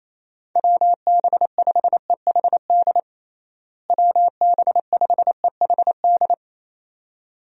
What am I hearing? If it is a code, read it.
Morse code, heard as a keyed tone, about 28 wpm, 700 Hz. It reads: WB5EHD WB5EHD
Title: USA Call Signs - 6 Characters - Repeated 2x - 28wpm